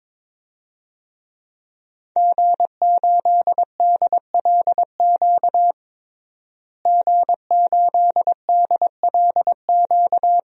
Morse code, heard as a keyed tone, about 21 wpm, 700 Hz. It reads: G8DLQ G8DLQ
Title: DXCC-100 Call Signs - 5 Characters - Repeated 2x - 22wpm